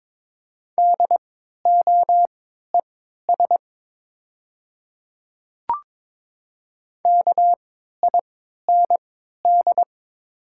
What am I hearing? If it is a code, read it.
Morse code, heard as a keyed tone, about 22 wpm, spaced as for 13 wpm, 700 Hz. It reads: DOES KIND
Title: 3x Character Spacing - Top 200 Words - 22wpm